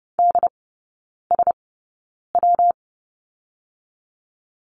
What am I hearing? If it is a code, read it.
Morse code, heard as a keyed tone, about 30 wpm, 700 Hz. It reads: D S W
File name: Lesson 19e - Letter w - Three Characters 30wpm